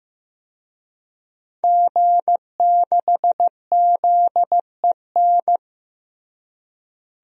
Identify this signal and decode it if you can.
Morse code, heard as a keyed tone, about 16 wpm, 700 Hz. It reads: G6ZEN
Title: DXCC-100 Call Signs - 5 Characters - Repeated 3x - 15wpm